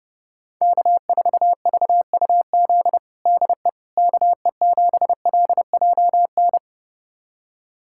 Morse code, heard as a keyed tone, about 29 wpm, 700 Hz. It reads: K4VUZ DE KE7LJN
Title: USA Callsign DE Callsign 30wpm